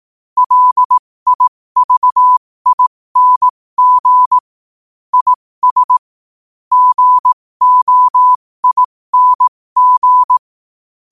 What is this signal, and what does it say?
Morse code, heard as a keyed tone, about 18 wpm, spaced as for 15 wpm, 1000 Hz. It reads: LIVING IS GOING
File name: qotdfiles / 15wpm